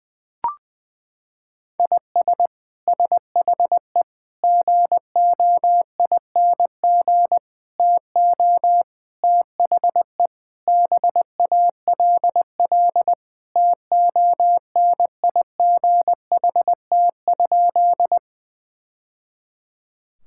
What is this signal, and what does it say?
Morse code, heard as a keyed tone, about 21 wpm, 700 Hz. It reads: IS SHE GOING TO THE BALL TONIGHT?
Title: Sentences from Top 1000 Words - Review Part 6 - 20wpm